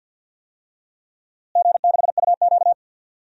Morse code, heard as a keyed tone, about 46 wpm, 700 Hz. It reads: G6UQ
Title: DXCC-100 Call Signs - 4 Characters - Repeated 2x - 50wpm